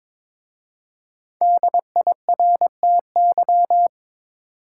Morse code, heard as a keyed tone, about 22 wpm, 700 Hz. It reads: DIRTY
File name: 5 Letter Words - First 3 Letters Given - 22wpm